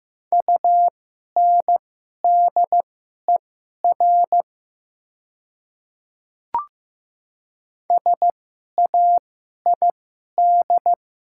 Morse code, heard as a keyed tone, about 15 wpm, spaced as for 11 wpm, 700 Hz. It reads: UNDER SAID
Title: 2x Character Spacing - Top 300 Words - 15wpm